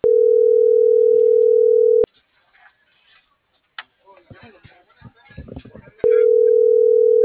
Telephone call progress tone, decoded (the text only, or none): ringback tone